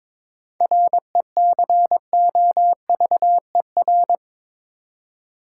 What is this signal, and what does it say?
Morse code, heard as a keyed tone, about 21 wpm, 700 Hz. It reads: RECOVER
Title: Seven Letter Words - 22wpm